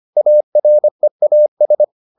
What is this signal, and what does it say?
Morse code, heard as a keyed tone, about 25 wpm, 600 Hz. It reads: AREAS